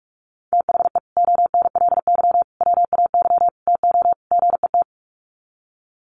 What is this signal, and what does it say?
Morse code, heard as a keyed tone, about 44 wpm, 700 Hz. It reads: THE ONLY WAY TO GET
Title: Sets of 5 Words Encore2 45wpm